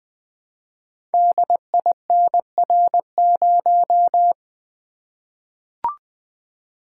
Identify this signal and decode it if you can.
Morse code, heard as a keyed tone, about 20 wpm, 700 Hz. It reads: DINR0
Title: Code Groups - 5 Letter:Numbers - 20wpm